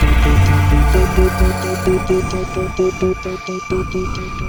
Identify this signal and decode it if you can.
There is no code.